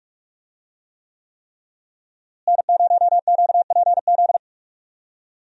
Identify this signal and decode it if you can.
Morse code, heard as a keyed tone, about 46 wpm, 700 Hz. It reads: N0QPZ
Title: USA Call Signs - 5 Characters - Repeated 3x - 45wpm